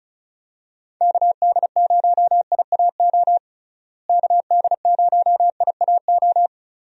Morse code, heard as a keyed tone, about 34 wpm, 700 Hz. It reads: KD0IAO KD0IAO
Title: USA Call Signs - 6 Characters - Repeated 3x - 35wpm